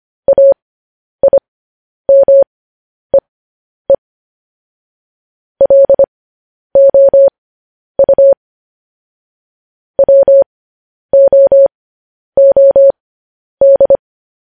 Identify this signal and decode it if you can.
Morse code, heard as a keyed tone, about 25 wpm, spaced as for 10 wpm, 550 Hz. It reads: AIMEE LOU WOOD
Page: news headlines in morse code at 10 wpm